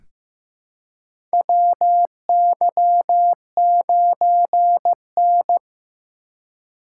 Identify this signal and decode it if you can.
Morse code, heard as a keyed tone, about 15 wpm, 700 Hz. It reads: WY9N
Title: USA Call Signs - 4 Characters - 3x Character Spacing - 15wpm